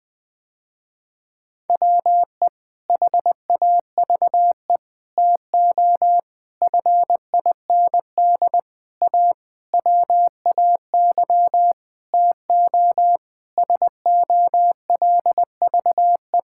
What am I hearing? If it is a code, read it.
Morse code, heard as a keyed tone, about 20 wpm, 700 Hz. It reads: WE HAVE TO FIND A WAY TO SOLVE
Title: Sentences with 6-Word Phrases - Spoken Beginning - 20wpm